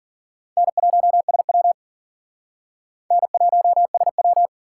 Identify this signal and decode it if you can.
Morse code, heard as a keyed tone, about 54 wpm, 700 Hz. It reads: N1SW N1SW